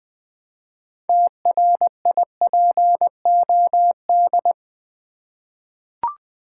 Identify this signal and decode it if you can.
Morse code, heard as a keyed tone, about 20 wpm, 700 Hz. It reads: TRIPOD
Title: Easy Words with Hints - 20wpm